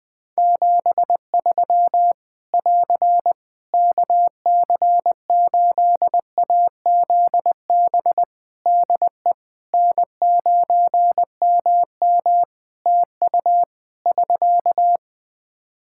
Morse code, heard as a keyed tone, about 20 wpm, 700 Hz. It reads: 73 <AR> KC8AZB DE N9MM TU <SK>
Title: K1USN SST QSO Practice 20wpm